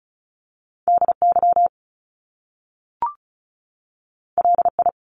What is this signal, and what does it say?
Morse code, heard as a keyed tone, about 36 wpm, 700 Hz. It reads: DY LI
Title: Top Two Letter Combinations in Words 35wpm